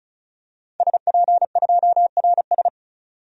Morse code, heard as a keyed tone, about 34 wpm, 700 Hz. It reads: SP2RS
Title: DXCC-100 Call Signs - 5 Characters - 35wpm